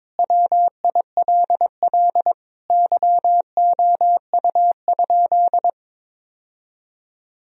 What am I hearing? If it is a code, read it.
Morse code, heard as a keyed tone, about 22 wpm, 700 Hz. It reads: WILL YOU?
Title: Sentences from Top 1000 Words - Review Part 4 - 22wpm